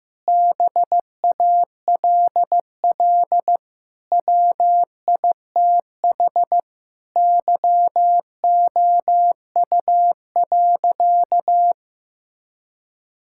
Morse code, heard as a keyed tone, about 16 wpm, 700 Hz. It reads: BALL WITH YOU.
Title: Sentences from Top 1000 Words - Review Part 6 - 15wpm